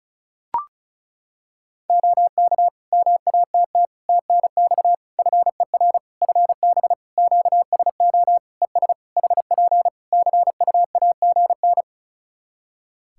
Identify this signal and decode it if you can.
Morse code, heard as a keyed tone, about 34 wpm, 700 Hz. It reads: OK MATT TNX FER FB QSO ES HP CUAGN
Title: Practice QSO - Exchange 4 - Ending - 35wpm